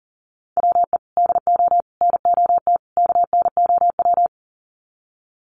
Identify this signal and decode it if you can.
Morse code, heard as a keyed tone, about 40 wpm, 700 Hz. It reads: WE DO NOT KNOW
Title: Sets of 4 Words 40wpm